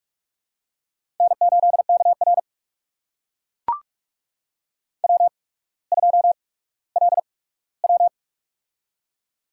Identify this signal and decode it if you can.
Morse code, heard as a keyed tone, about 44 wpm, spaced as for 19 wpm, 700 Hz. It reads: N8KR W2LW
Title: USA Call Signs - 4 Characters - 8x Character Spacing - 45wpm